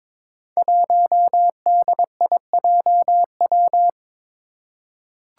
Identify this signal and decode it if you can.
Morse code, heard as a keyed tone, about 22 wpm, 700 Hz. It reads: 1DIJW
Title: Five Challenging Letter-Number Combinations 22wpm